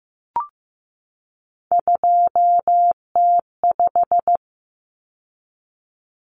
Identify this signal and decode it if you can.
Morse code, heard as a keyed tone, about 15 wpm, 700 Hz. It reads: 2T5
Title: Common Cut Numbers - 3-Digit 15wpm